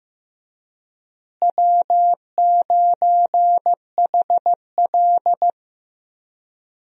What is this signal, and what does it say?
Morse code, heard as a keyed tone, about 15 wpm, 700 Hz. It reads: W9HL
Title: USA Call Signs - 4 Characters - Repeated 2x - 15wpm